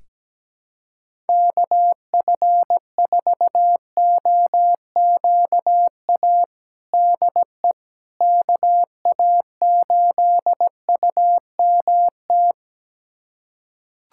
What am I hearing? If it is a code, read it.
Morse code, heard as a keyed tone, about 17 wpm, 700 Hz. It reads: KF4OQA DE KA8UMT
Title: USA Callsign DE Callsign - 6 Characters - 17wpm